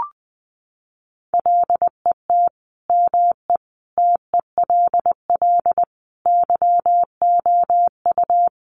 Morse code, heard as a keyed tone, about 20 wpm, 700 Hz. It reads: LET ME TELL YOU
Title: Sets of 5 Words 20wpm